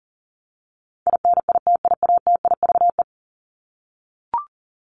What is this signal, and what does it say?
Morse code, heard as a keyed tone, about 40 wpm, 700 Hz. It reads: INITIATIVE
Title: Top 1751-2000 Words 40wpm